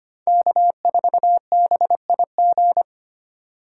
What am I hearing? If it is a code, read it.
Morse code, heard as a keyed tone, about 25 wpm, 700 Hz. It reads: K4BIG
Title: USA Call Signs - 5 Characters - 2x Character Spacing - 25wpm